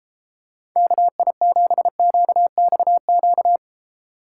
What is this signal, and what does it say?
Morse code, heard as a keyed tone, about 33 wpm, 700 Hz. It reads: KI7QXQ